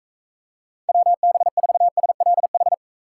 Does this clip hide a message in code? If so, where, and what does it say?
Yes: Morse code from 0.9 to 2.7 s, heard as a keyed tone, about 43 wpm, 700 Hz. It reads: WB4SLH